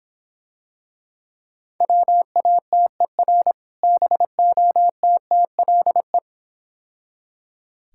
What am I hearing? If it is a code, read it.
Morse code, heard as a keyed tone, about 26 wpm, 700 Hz. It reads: WATER BOTTLE